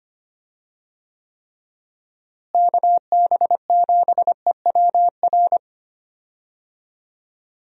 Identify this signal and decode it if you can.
Morse code, heard as a keyed tone, about 25 wpm, 700 Hz. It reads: KB7EWR
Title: USA Call Signs - 6 Characters - Repeated 2x - 25wpm